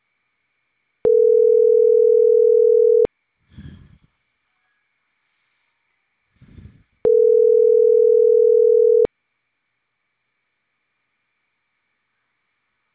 Ringback tone (North America), 440 + 480 Hz.